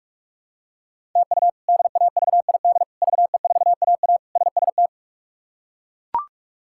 Morse code, heard as a keyed tone, about 41 wpm, 700 Hz. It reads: TU DAVID VE4AA SST
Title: K1USN SST QSO Practice - 45wpm